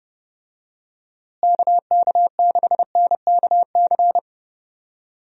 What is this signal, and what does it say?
Morse code, heard as a keyed tone, about 30 wpm, 700 Hz. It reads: KK6NKC
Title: USA Call Signs - 6 Characters - Repeated 3x - 30wpm